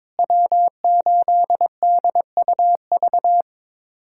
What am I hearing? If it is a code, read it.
Morse code, heard as a keyed tone, about 22 wpm, 700 Hz. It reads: W8DUV
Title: USA Callsign DE Callsign 22wpm